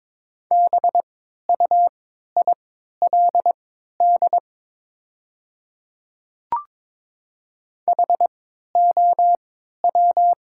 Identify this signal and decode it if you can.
Morse code, heard as a keyed tone, about 22 wpm, spaced as for 12 wpm, 690 Hz. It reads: BUILD HOW
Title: 3x Character Spacing - Top 200 Words - 22wpm